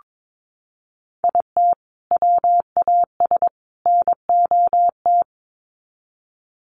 Morse code, heard as a keyed tone, about 22 wpm, 700 Hz. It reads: IT WAS NOT
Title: Sets of 3 Words 22wpm